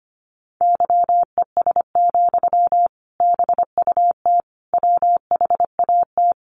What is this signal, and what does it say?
Morse code, heard as a keyed tone, about 25 wpm, 700 Hz. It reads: YES, BUT WHAT
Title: Sentences from Top 500 Words 25wpm